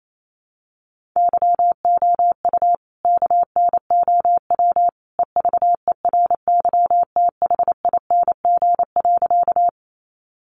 Morse code, heard as a keyed tone, about 28 wpm, 700 Hz. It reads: YOU KNOW EVERYTHING.
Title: Sentences from Top 700 Words 28wpm